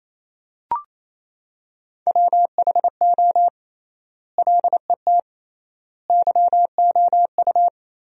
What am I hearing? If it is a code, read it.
Morse code, heard as a keyed tone, about 27 wpm, 700 Hz. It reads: WHO LET YOU
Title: 3x Word-Spacing - Sentences from Top 300 Words - 28wpm